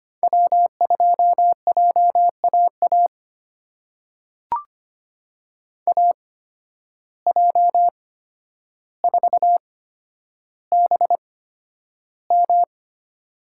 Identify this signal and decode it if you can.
Morse code, heard as a keyed tone, about 25 wpm, spaced as for 11 wpm, 700 Hz. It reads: W2JAA AJ4BM
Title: USA Call Signs - 5 Characters - 8x Character Spacing - 25wpm